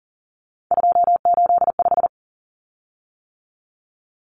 Morse code, heard as a keyed tone, about 40 wpm, 700 Hz. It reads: 285
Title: Common Cut Numbers - 3-Digit 40wpm